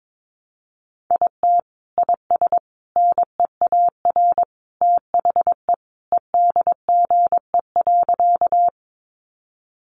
Morse code, heard as a keyed tone, about 22 wpm, 700 Hz. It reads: IT IS NEAR THE EDGE.
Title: Additional Easy Phrases 22wpm